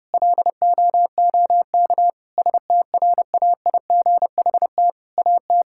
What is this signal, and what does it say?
Morse code, heard as a keyed tone, about 32 wpm, 700 Hz. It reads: LOOK STRAIGHT AT
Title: Sentences from Top 1000 Words - Review Part 6 - 30wpm